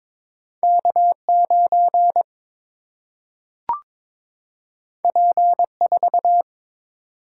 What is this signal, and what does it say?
Morse code, heard as a keyed tone, about 22 wpm, 700 Hz. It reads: K9 P4
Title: Single-Letter Single-Number 22wpm